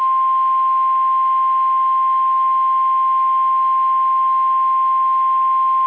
Emergency Alert System (SAME) tones (the 1050 Hz attention tone). It is heard from 0.6 to 5.1 s.
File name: eas / KSDZKDJL / KSDZKDJL